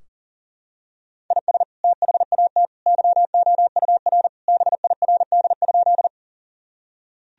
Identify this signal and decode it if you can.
Morse code, heard as a keyed tone, about 40 wpm, 700 Hz. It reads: IS THAT YOUR BIRD?
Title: Sentences from Top 1000 Words - Review Part 2 - 40wpm